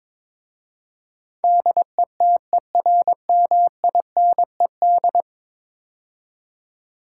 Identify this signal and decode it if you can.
Morse code, heard as a keyed tone, about 22 wpm, 700 Hz. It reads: DETERMINED
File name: practice / 22wpm / 10 Letter Words - First 3 Letters Given - 22wpm